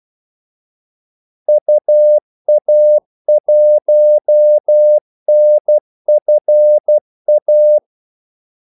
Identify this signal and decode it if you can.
Morse code, heard as a keyed tone, about 12 wpm, 600 Hz. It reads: UA1NFA
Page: 2 ekstra ordmellemrum mellem kaldesignalerne